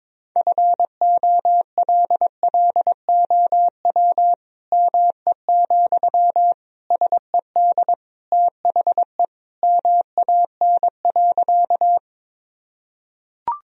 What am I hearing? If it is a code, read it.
Morse code, heard as a keyed tone, about 22 wpm, 700 Hz. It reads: FOLLOW ME, SED THE MAN.
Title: Sentences from Top 200 Words and Abbreviations 22wpm